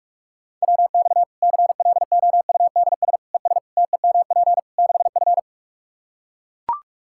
Morse code, heard as a keyed tone, about 43 wpm, 700 Hz. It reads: WX CLOUDS ES TEMP 6F
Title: Practice QSO - Exchange 2 - Info - 45wpm